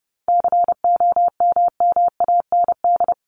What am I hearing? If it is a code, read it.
Morse code, heard as a keyed tone, about 30 wpm, 700 Hz. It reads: COMMAND
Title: Top 1501-1750 Words 30wpm